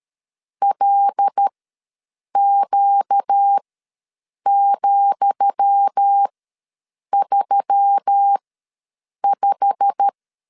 Morse code, heard as a keyed tone, about 13 wpm, spaced as for 7 wpm, 800 Hz. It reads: LQ,35
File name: LEZIONI / 10 SIGNS / 10D COMA ws 12wpm 3618